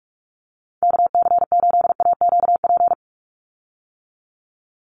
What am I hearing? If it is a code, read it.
Morse code, heard as a keyed tone, about 44 wpm, 700 Hz. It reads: KC8AQP
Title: USA Call Signs 45wpm